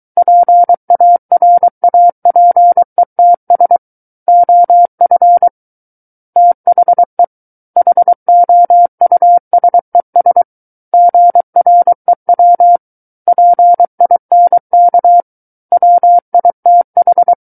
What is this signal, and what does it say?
Morse code, heard as a keyed tone, about 23 wpm, 700 Hz. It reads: PARAPETS OF THE HOUSES GREW PINK WITH
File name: Book One / 14-Chapter 14 - 23 WPM